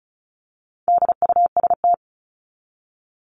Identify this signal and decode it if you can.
Morse code, heard as a keyed tone, about 34 wpm, 700 Hz. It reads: DUST
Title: Top 1751-2000 Words 35wpm